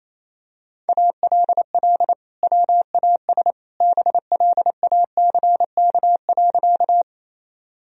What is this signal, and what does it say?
Morse code, heard as a keyed tone, about 27 wpm, 700 Hz. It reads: ALL WAS BLACK.
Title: Sentences from Top 1000 Words - Review Part 2 - 28wpm